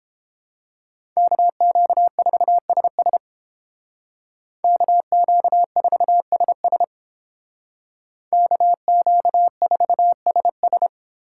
Morse code, heard as a keyed tone, about 29 wpm, 700 Hz. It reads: KQ4SS KQ4SS KQ4SS